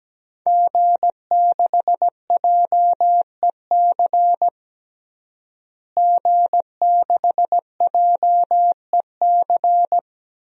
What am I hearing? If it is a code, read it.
Morse code, heard as a keyed tone, about 17 wpm, 700 Hz. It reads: G6JEC G6JEC
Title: DXCC-100 Call Signs - 5 Characters - Repeated 2x - 17wpm